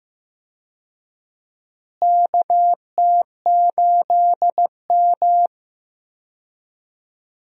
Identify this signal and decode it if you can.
Morse code, heard as a keyed tone, about 15 wpm, 700 Hz. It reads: KT8M
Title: USA Call Signs - 4 Characters - Repeated 3x - 15wpm